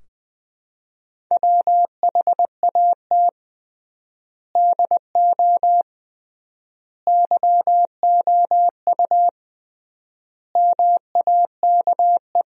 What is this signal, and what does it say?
Morse code, heard as a keyed tone, about 20 wpm, 700 Hz. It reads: WHAT DO YOU MAKE
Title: 3x Word-Spacing - Sentences from Top 100 Words - 20wpm